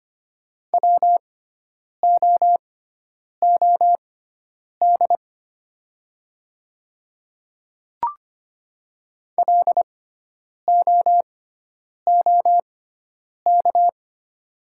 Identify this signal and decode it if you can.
Morse code, heard as a keyed tone, about 25 wpm, spaced as for 9 wpm, 690 Hz. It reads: WOOD LOOK
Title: 6x Character Spacing - Top 300 Words - 25wpm